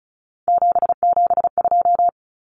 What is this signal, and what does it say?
Morse code, heard as a keyed tone, about 34 wpm, 700 Hz. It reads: 772
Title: Numbers - 3-Digit 35wpm